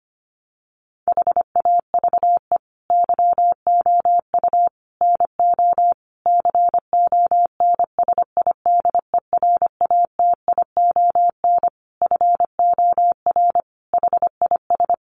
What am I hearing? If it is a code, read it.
Morse code, heard as a keyed tone, about 25 wpm, 700 Hz. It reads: HAVE YOU NO CONSIDERATION FOR HIS
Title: Sentences from Top 1500 Words 25wpm